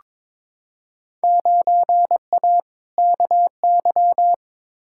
Morse code, heard as a keyed tone, about 22 wpm, 700 Hz. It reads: 9A KY
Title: Field Day Exchanges Encore - 22wpm